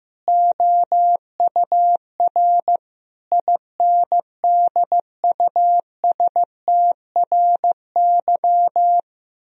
Morse code, heard as a keyed tone, about 15 wpm, 700 Hz. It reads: OUR INDUSTRY
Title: Sentences from Top 1000 Words - Review Part 4 - 15wpm